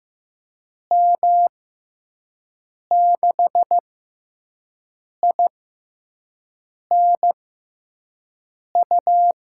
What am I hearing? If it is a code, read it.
Morse code, heard as a keyed tone, about 15 wpm, spaced as for 5 wpm, 700 Hz. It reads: M6INU
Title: USA Call Signs - 6 Characters - 6x Character Spacing - 15wpm